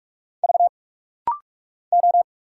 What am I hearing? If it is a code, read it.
Morse code, heard as a keyed tone, about 46 wpm, 700 Hz. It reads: V O
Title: Single Letters - Rapid-Fire 45wpm